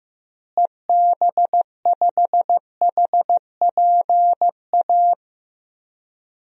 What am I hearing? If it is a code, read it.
Morse code, heard as a keyed tone, about 15 wpm, 700 Hz. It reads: EB5HPA
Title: DXCC-100 Call Signs - 6 Characters - Repeated 3x - 15wpm